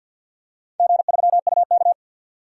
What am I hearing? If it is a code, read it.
Morse code, heard as a keyed tone, about 52 wpm, 700 Hz. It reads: G3UK
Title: DXCC-100 Call Signs - 4 Characters - 50wpm